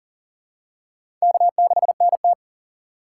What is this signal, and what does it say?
Morse code, heard as a keyed tone, about 40 wpm, 700 Hz. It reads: K6NT
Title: USA Call Signs - 4 Characters - 4x Character Spacing - 40wpm